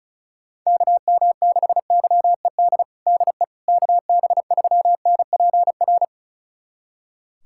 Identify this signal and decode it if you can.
Morse code, heard as a keyed tone, about 36 wpm, 700 Hz. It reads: KM6YED DE KB3NPR
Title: USA Callsign DE Callsign - 6 Characters - 35wpm